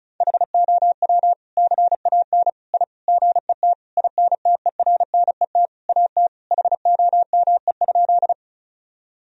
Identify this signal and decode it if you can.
Morse code, heard as a keyed tone, about 34 wpm, 700 Hz. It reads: HOW CAN I GET INTERNET AT HOME?
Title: Sentences from Top 1000 Words - Review Part 4 - 35wpm